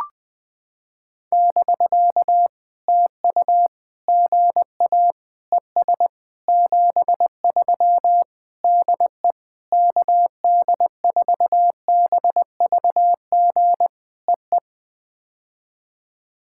Morse code, heard as a keyed tone, about 20 wpm, 700 Hz. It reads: <BK> TU GA ES 73 DE KD4BVG EE
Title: POTA QSO Practice - 20wpm